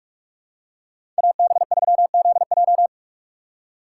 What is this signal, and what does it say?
Morse code, heard as a keyed tone, about 42 wpm, 700 Hz. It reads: AB3ZJ
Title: USA Call Signs - 5 Characters - 6x Character Spacing - 45wpm